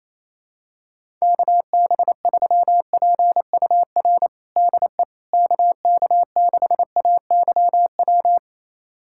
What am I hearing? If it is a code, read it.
Morse code, heard as a keyed tone, about 28 wpm, 700 Hz. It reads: KB3PUR DE KK6AYW
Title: USA Callsign DE Callsign - 6 Characters - 28wpm